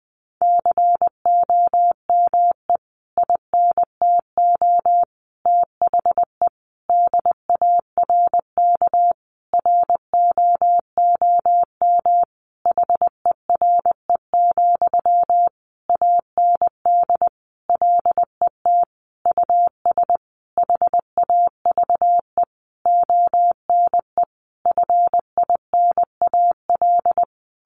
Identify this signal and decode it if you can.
Morse code, heard as a keyed tone, about 20 wpm, 700 Hz. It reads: COME INTO THE DARK ROOM HERE, AND LET US HAVE ONE FINAL
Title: Sentences from Top 1000 Words 20wpm